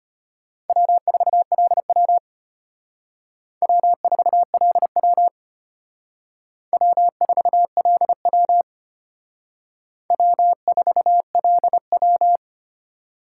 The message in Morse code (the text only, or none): W4LW W4LW W4LW W4LW